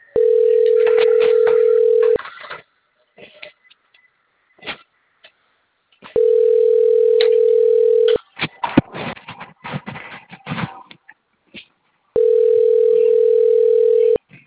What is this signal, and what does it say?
Ringback tone (North America), 440 + 480 Hz.
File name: rg-1600-unknown- - - .45443